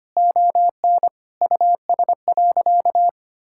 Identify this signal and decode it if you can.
Morse code, heard as a keyed tone, about 25 wpm, 700 Hz. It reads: ON US.
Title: Sentences from Top 1600 Words 25wpm